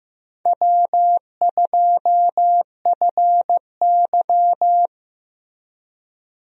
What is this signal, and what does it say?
Morse code, heard as a keyed tone, about 15 wpm, 700 Hz. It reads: W2FY
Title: USA Call Signs - 4 Characters - Repeated 3x - 15wpm